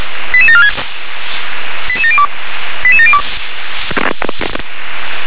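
Five-tone selective call (ZVEI), heard as quick runs of five tones, tones of about 80 ms. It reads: 88746 88702 88702
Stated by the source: ZVEI five-tone selective call received by SDR.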